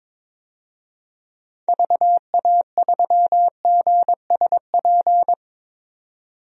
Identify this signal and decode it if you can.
Morse code, heard as a keyed tone, about 22 wpm, 700 Hz. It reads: VA3GSP
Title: DXCC-100 Call Signs - 6 Characters - 22wpm